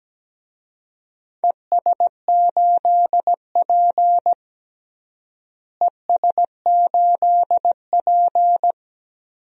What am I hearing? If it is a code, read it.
Morse code, heard as a keyed tone, about 17 wpm, 700 Hz. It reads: ES8P ES8P
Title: DXCC-100 Call Signs - 4 Characters - Repeated 2x - 17wpm